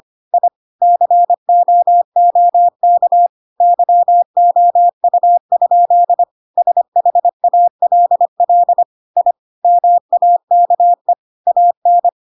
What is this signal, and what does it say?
Morse code, heard as a keyed tone, about 25 wpm, 700 Hz. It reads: I COOK YOU? SHALL I MAKE AN
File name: Pinocchio / Pinocchio ch5 25 25 ik1hge